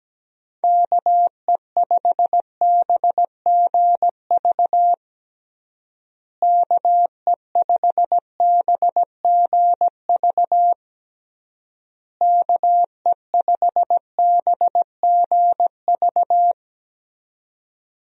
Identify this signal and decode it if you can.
Morse code, heard as a keyed tone, about 17 wpm, 700 Hz. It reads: KE5BGV KE5BGV KE5BGV
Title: USA Call Signs - 6 Characters - Repeated 3x - 17wpm